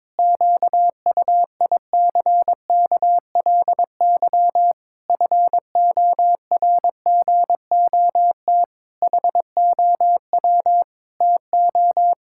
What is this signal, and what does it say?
Morse code, heard as a keyed tone, about 22 wpm, 700 Hz. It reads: QUICKLY FORGOT HOW TO
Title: Pangram Challenge v2 - 22wpm